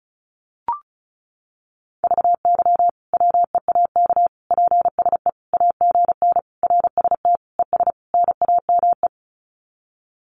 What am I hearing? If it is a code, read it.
Morse code, heard as a keyed tone, about 34 wpm, 700 Hz. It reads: VY WEAK PSE AGN RST ES NAME
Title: Common QSO Phrases 35wpm